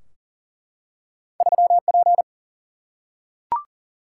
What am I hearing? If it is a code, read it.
Morse code, heard as a keyed tone, about 40 wpm, 700 Hz. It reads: 3 P
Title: Single-Letter Single-Number 40wpm